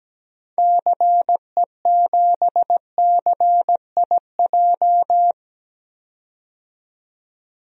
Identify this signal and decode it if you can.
Morse code, heard as a keyed tone, about 17 wpm, 700 Hz. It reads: CE7CIJ
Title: DXCC-100 Call Signs - 6 Characters - 17wpm